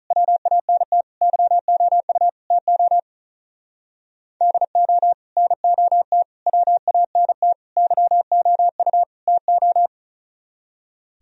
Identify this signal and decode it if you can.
Morse code, heard as a keyed tone, about 46 wpm, 700 Hz. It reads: WANT YOU TO DO NOT WANT YOU TO